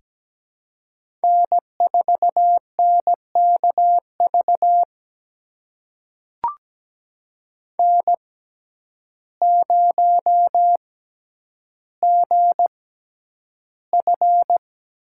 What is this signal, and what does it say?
Morse code, heard as a keyed tone, about 17 wpm, spaced as for 9 wpm, 700 Hz. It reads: N4NKV N0GF
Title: USA Call Signs - 5 Characters - 6x Character Spacing - 17wpm